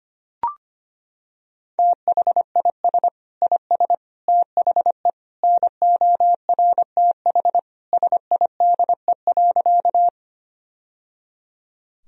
Morse code, heard as a keyed tone, about 25 wpm, 700 Hz. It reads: THIS IS THE NORTH SIDE.
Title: Sentences from Top 1000 Words - Review Part 1 - 25wpm